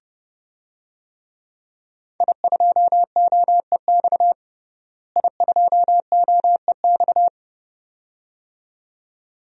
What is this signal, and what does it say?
Morse code, heard as a keyed tone, about 30 wpm, 700 Hz. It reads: I2OEX I2OEX
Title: DXCC-100 Call Signs - 5 Characters - Repeated 2x - 30wpm